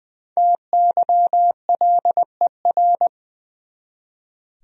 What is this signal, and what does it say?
Morse code, heard as a keyed tone, about 20 wpm, 700 Hz. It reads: TYLER
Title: USA Common Names - Max 5 Letters - 20wpm